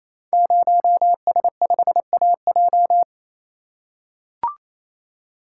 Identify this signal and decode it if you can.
Morse code, heard as a keyed tone, about 29 wpm, 700 Hz. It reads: 0S5AJ
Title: Code Groups - 5 Letter:Numbers - 28wpm